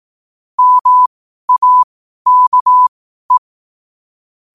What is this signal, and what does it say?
Morse code, heard as a keyed tone, about 18 wpm, spaced as for 12 wpm, 1000 Hz. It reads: MAKE
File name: qotdfiles / 12wpm